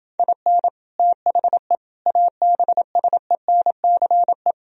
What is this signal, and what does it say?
Morse code, heard as a keyed tone, about 28 wpm, 700 Hz. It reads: IN THE ABSENCE